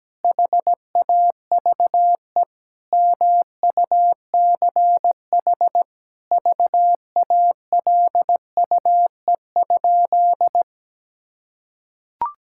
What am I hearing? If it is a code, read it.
Morse code, heard as a keyed tone, about 17 wpm, 700 Hz. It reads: HAVE MUCH VALUE?
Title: Sentences from Top 1000 Words - Review Part 6 - 17wpm